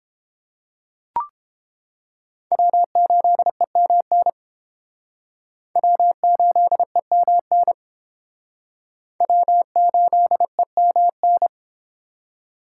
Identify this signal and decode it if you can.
Morse code, heard as a keyed tone, about 30 wpm, 700 Hz. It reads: W8EMN W8EMN W8EMN